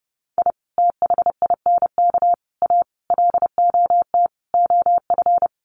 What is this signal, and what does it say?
Morse code, heard as a keyed tone, about 30 wpm, 700 Hz. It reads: I THINK A LOT OF
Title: Sets of 5 Words 30wpm